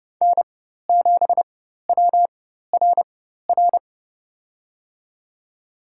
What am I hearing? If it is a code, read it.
Morse code, heard as a keyed tone, about 30 wpm, spaced as for 14 wpm, 700 Hz. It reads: N7WRR
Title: USA Call Signs - 5 Characters - 4x Character Spacing - 30wpm